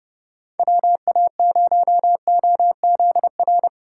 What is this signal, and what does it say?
Morse code, heard as a keyed tone, about 31 wpm, 700 Hz. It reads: WA0OZR
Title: USA Call Signs - 6 Characters - 6x Character Spacing - 30wpm